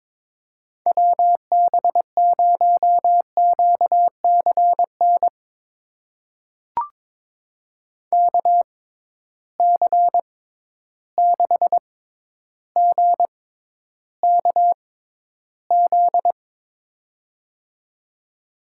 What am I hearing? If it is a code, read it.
Morse code, heard as a keyed tone, about 22 wpm, spaced as for 11 wpm, 700 Hz. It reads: WB0QCN KC6GKZ
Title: USA Call Signs - 6 Characters - 6x Character Spacing - 22wpm